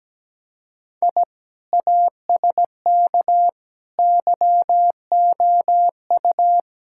Morse code, heard as a keyed tone, about 17 wpm, 700 Hz. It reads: I ASK YOU
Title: Sentences from Top 1000 Words - Review Part 5 - 17wpm